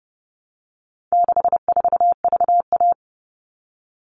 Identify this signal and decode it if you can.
Morse code, heard as a keyed tone, about 30 wpm, 700 Hz. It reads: 64VA
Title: All Cut Numbers - 4-Digit 30wpm